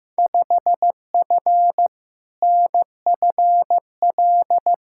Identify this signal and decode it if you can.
Morse code, heard as a keyed tone, about 15 wpm, 700 Hz. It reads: 5F NFL
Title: Field Day Exchanges Encore - 15wpm